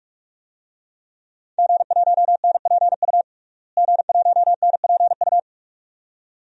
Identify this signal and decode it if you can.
Morse code, heard as a keyed tone, about 46 wpm, 700 Hz. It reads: G1NPU G1NPU
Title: DXCC-100 Call Signs - 5 Characters - Repeated 2x - 45wpm